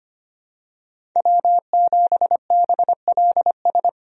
Morse code, heard as a keyed tone, about 25 wpm, 700 Hz. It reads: W7BLS
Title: USA Call Signs - 5 Characters - 25wpm